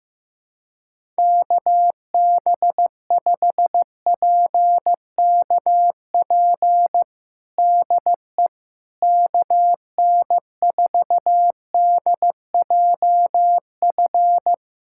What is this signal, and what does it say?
Morse code, heard as a keyed tone, about 15 wpm, 700 Hz. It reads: KB5PKP DE KN4DJF
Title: USA Callsign DE Callsign - 6 Characters - 15wpm